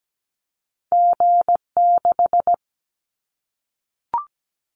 Morse code, heard as a keyed tone, about 17 wpm, 700 Hz. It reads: G6
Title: Callsign Prefixes for Top 25 DXCC - Repeated 3x - 17wpm